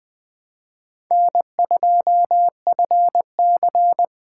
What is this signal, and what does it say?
Morse code, heard as a keyed tone, about 21 wpm, 700 Hz. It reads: N2FC